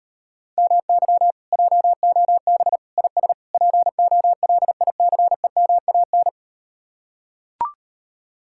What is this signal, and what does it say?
Morse code, heard as a keyed tone, about 39 wpm, 700 Hz. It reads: MY JOB IS POLICEMAN